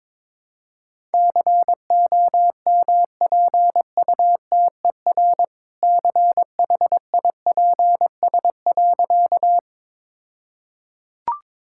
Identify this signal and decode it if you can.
Morse code, heard as a keyed tone, about 21 wpm, 700 Hz. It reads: COMPUTER CHIPS.
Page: morse-code-ninja-practice-22wpm